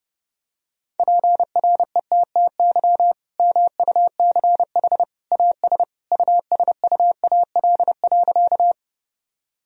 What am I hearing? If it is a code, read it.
Morse code, heard as a keyed tone, about 30 wpm, 700 Hz. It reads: PRETTY MUCH AS USUAL.
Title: Sentences from Top 1000 Words - Review Part 2 - 30wpm